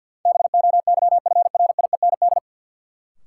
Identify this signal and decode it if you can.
Morse code, heard as a keyed tone, about 51 wpm, 700 Hz. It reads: BOYFRIEND
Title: Sentences from Top 1000 Words - Review Part 5 - New Word Drill - 50wpm